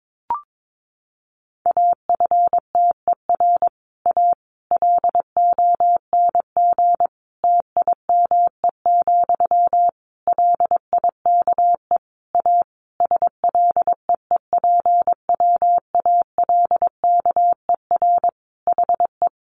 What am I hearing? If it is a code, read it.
Morse code, heard as a keyed tone, about 22 wpm, 700 Hz. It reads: AFTER A LONG TIME, LIKE A SLEEPWALKER HE
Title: A Sound Of Thunder 22wpm